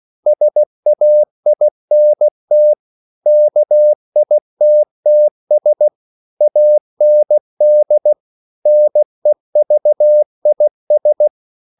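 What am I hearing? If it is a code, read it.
Morse code, heard as a keyed tone, about 16 wpm, 600 Hz. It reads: SAINT KITTS AND NEVIS
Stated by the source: CW practice of national capitals at 16 words per minute, from ON6ZQ.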